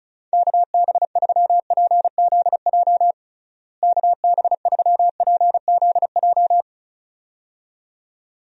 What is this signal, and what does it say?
Morse code, heard as a keyed tone, about 36 wpm, 700 Hz. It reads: KB3PZJ KB3PZJ
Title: USA Call Signs - 6 Characters - Repeated 2x - 35wpm